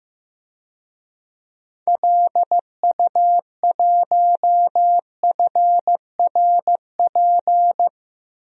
Morse code, heard as a keyed tone, about 15 wpm, 700 Hz. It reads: LU1FRP
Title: DXCC-100 Call Signs - 6 Characters - Repeated 2x - 15wpm